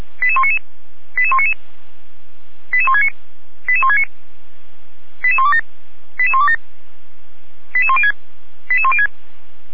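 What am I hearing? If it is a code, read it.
Five-tone selective call (ZVEI), heard as quick runs of five tones, tones of about 70 ms. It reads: 88199 88199 88169 88169 88127 88127 88106 88106
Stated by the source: Five-tone ZVEI selective call 88129, received on an SDR, from the Feuerwehr Orlamünde fire brigade alerting archive.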